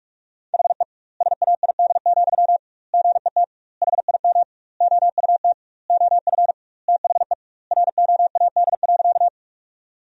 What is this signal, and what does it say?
Morse code, heard as a keyed tone, about 45 wpm, 700 Hz. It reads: HE SAID, GET HIM OUT OF THE ROAD.
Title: 2x Word-Spacing - Sentences from Top 400 Words - 45wpm